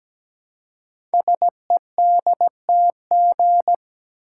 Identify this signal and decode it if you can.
Morse code, heard as a keyed tone, about 17 wpm, 700 Hz. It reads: SEDTG
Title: Code Groups - 5 Letter:Numbers - 17wpm